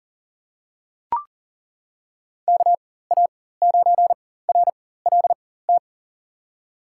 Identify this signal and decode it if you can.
Morse code, heard as a keyed tone, about 40 wpm, spaced as for 19 wpm, 700 Hz. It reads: KA9RLT
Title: USA Call Signs - 6 Characters - 4x Character Spacing - 40wpm